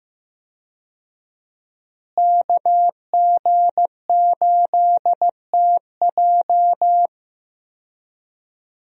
Morse code, heard as a keyed tone, about 15 wpm, 700 Hz. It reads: KG8TJ
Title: USA Call Signs - 5 Characters - Repeated 2x - 15wpm